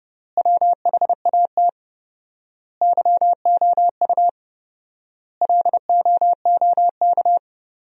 Morse code, heard as a keyed tone, about 32 wpm, 700 Hz. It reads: WHAT YOU LOOK
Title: 4x Word-Spacing - Sentences from Top 100 Words - 30wpm